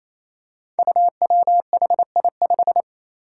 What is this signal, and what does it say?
Morse code, heard as a keyed tone, about 28 wpm, 700 Hz. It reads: UWHI5
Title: Five Challenging Letter-Number Combinations 28wpm